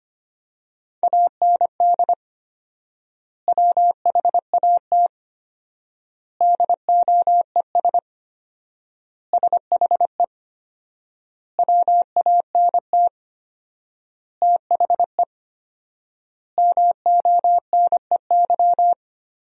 Morse code, heard as a keyed tone, about 25 wpm, 700 Hz. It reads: AND WHAT DOES SHE WANT THE MONEY
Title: 4x Word-Spacing - Sentences from Top 400 Words - 25wpm